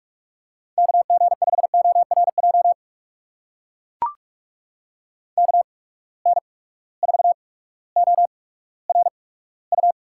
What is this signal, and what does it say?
Morse code, heard as a keyed tone, about 44 wpm, spaced as for 19 wpm, 700 Hz. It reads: KG5ORJ KN4ORU
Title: USA Call Signs - 6 Characters - 8x Character Spacing - 45wpm